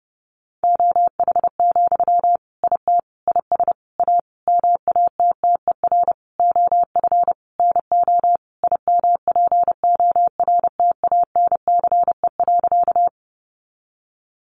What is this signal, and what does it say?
Morse code, heard as a keyed tone, about 30 wpm, 700 Hz. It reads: OH, IT IS A MATTER OF NO IMPORTANCE.
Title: Sentences from Top 1200 Words 30wpm